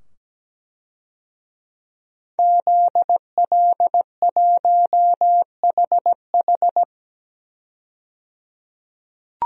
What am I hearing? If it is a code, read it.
Morse code, heard as a keyed tone, about 17 wpm, 700 Hz. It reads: ZL1HH
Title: DXCC-100 Call Signs - 5 Characters - 17wpm